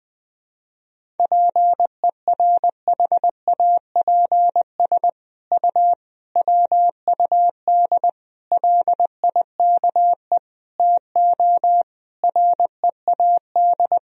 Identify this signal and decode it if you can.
Morse code, heard as a keyed tone, about 20 wpm, 700 Hz. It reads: PERHAPS U WUD LIKE TO READ
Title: Sentences from Top 500 Words and Abbreviations 20wpm